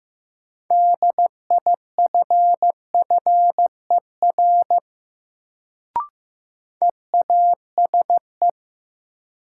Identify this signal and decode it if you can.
Morse code, heard as a keyed tone, about 15 wpm, 700 Hz. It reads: DIFFER EASE
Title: Sentences from Top 1000 Words - Review Part 1 - New Word Drill - 15wpm